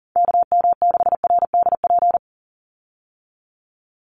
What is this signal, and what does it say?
Morse code, heard as a keyed tone, about 40 wpm, 700 Hz. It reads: KM6RDP
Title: USA Call Signs 40wpm